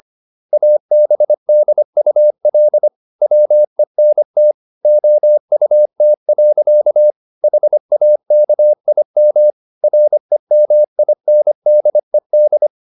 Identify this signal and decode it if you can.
Morse code, heard as a keyed tone, about 25 wpm, 600 Hz. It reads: ABDUL WENT OUT. HAKIM REMINDED
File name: Never 0025wpm